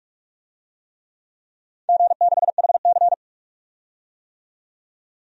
Morse code, heard as a keyed tone, about 44 wpm, 700 Hz. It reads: G6HC
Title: DXCC-100 Call Signs - 4 Characters - 45wpm